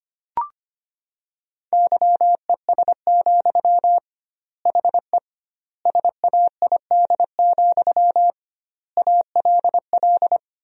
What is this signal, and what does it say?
Morse code, heard as a keyed tone, about 25 wpm, 700 Hz. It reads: YES, HE SAID, ALL
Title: 2x Word-Spacing - Sentences from Top 500 Words - 25wpm